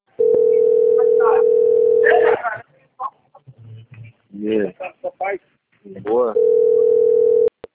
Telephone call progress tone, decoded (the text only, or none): ringback tone